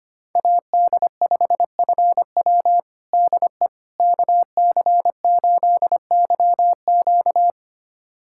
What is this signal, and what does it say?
Morse code, heard as a keyed tone, about 25 wpm, 700 Hz. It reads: AD5FW DE KC8YQ
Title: USA Callsign DE Callsign - 5 Characters - 25wpm